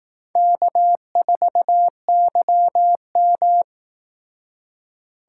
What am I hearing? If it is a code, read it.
Morse code, heard as a keyed tone, about 18 wpm, 700 Hz. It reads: K4YM
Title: Speed Racing - USA 1x2 Call Signs 15wpm